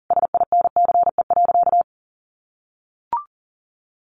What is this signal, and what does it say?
Morse code, heard as a keyed tone, about 40 wpm, 700 Hz. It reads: SINCE.
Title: Sentences from Top 700 Words 40wpm